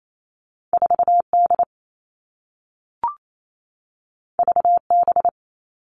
Morse code, heard as a keyed tone, about 28 wpm, 700 Hz. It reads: 4D VB
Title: All Cut Numbers - 2-Digit 28wpm